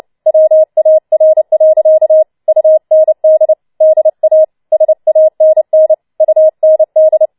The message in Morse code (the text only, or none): WAR. UND DA SANN UND